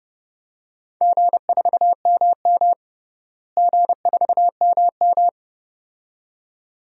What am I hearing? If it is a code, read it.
Morse code, heard as a keyed tone, about 30 wpm, 700 Hz. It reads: G4MM G4MM
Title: DXCC-100 Call Signs - 4 Characters - Repeated 2x - 30wpm